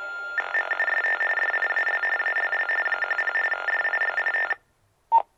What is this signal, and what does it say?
TTY/TDD text telephone tones, about 45.45 baud. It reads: MUST DIAL ALL 10 DIGIT
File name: fios fe dial area code